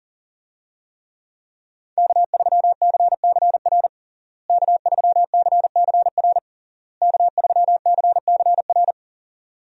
Morse code, heard as a keyed tone, about 40 wpm, 700 Hz. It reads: K3CCR K3CCR K3CCR
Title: USA Call Signs - 5 Characters - Repeated 3x - 40wpm